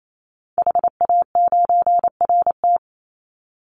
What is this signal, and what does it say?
Morse code, heard as a keyed tone, about 28 wpm, 700 Hz. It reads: HA9RT
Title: Call Signs - 28wpm